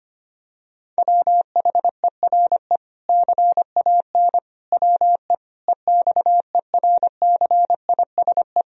Morse code, heard as a keyed tone, about 25 wpm, 700 Hz. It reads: WHERE CAN WE EXERCISE
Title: Sentences from Top 1000 Words - Review Part 4 - 25wpm